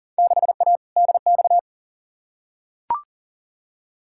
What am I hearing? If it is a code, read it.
Morse code, heard as a keyed tone, about 40 wpm, 700 Hz. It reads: 6A DX
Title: Field Day Exchanges Encore - 40wpm